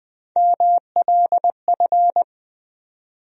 Morse code, heard as a keyed tone, about 20 wpm, 700 Hz. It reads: MLF
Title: USA Call Sign Postfixes - 3 Letters 20wpm